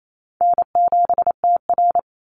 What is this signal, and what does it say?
Morse code, heard as a keyed tone, about 28 wpm, 700 Hz. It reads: N7TR
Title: Easy Call Signs - 28wpm